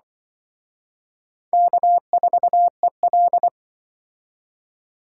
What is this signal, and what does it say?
Morse code, heard as a keyed tone, about 23 wpm, 700 Hz. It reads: K4EL
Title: Speed Racing - USA 1x2 Call Signs 20wpm